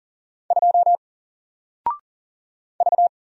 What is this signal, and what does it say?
Morse code, heard as a keyed tone, about 40 wpm, 700 Hz. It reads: 2 V
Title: Single Letter-Number Alt 40wpm